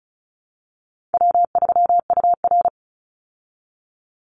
Morse code, heard as a keyed tone, about 34 wpm, 700 Hz. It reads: W3UR
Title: Easy Call Signs Repeated 3x - 35wpm